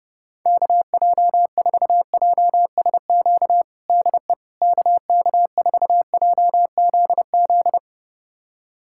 Morse code, heard as a keyed tone, about 30 wpm, 700 Hz. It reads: KJ4JSQ DE KK4JZZ
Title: USA Callsign DE Callsign 30wpm